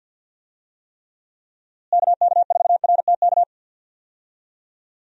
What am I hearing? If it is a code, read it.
Morse code, heard as a keyed tone, about 47 wpm, 700 Hz. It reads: KK4RTK
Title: USA Call Signs - 6 Characters - Repeated 2x - 50wpm